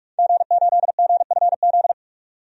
Morse code, heard as a keyed tone, about 46 wpm, 700 Hz. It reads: G8GFZ
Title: DXCC-100 Call Signs - 5 Characters - 45wpm